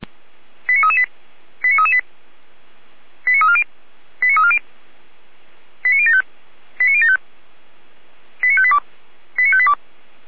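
Five-tone selective call (ZVEI), heard as quick runs of five tones, tones of about 80 ms. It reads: 89338 89338 89340 89340 89075 89075 89582 89582
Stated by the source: ZVEI five-tone selective call received on SDR.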